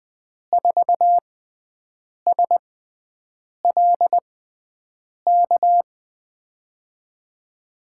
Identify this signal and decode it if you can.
Morse code, heard as a keyed tone, about 20 wpm, spaced as for 7 wpm, 700 Hz. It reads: 4SLK
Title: USA Call Signs - 6x Character Spacing - 20wpm